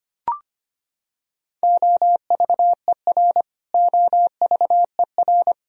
Morse code, heard as a keyed tone, about 25 wpm, 700 Hz. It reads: OVER OVER
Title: Top 100 Words Repeated 25wpm